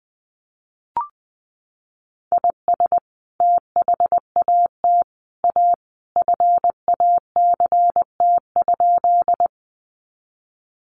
Morse code, heard as a keyed tone, about 20 wpm, 700 Hz. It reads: IS THAT A FACT?
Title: Sentences from Top 500 Words 20wpm